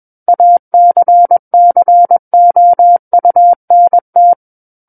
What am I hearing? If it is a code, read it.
Morse code, heard as a keyed tone, about 21 wpm, 700 Hz. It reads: ACCOUNT
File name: Book One / 12-Chapter 12 - 21 WPM